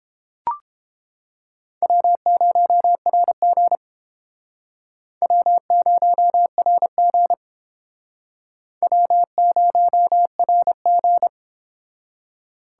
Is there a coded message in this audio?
Yes: Morse code from 1.8 to 11.3 s, heard as a keyed tone, about 31 wpm, 700 Hz. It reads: W0RG W0RG W0RG